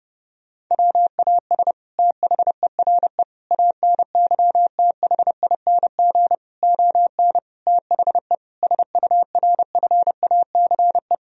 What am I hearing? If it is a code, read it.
Morse code, heard as a keyed tone, about 30 wpm, 700 Hz. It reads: WAS THERE ANYTHING ON THE SURFACE
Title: Sentences from Top 1000 Words - Review Part 3 - 30wpm